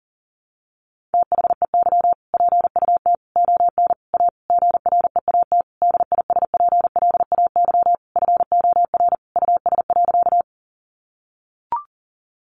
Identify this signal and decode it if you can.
Morse code, heard as a keyed tone, about 40 wpm, 700 Hz. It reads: THEY PUT ON A GREAT DISPLAY FOR US.
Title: Sentences from Top 1200 Words 40wpm